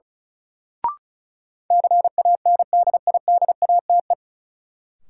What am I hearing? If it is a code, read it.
Morse code, heard as a keyed tone, about 34 wpm, 700 Hz. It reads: CANDIDATE
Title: Sentences from Top 1600 Words - New Word Drill - 35wpm